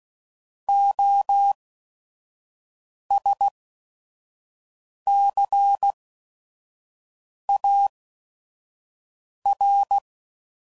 Morse code, heard as a keyed tone, about 16 wpm, spaced as for 5 wpm, 800 Hz. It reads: OSCAR
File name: Prefixes / By First Letter / Os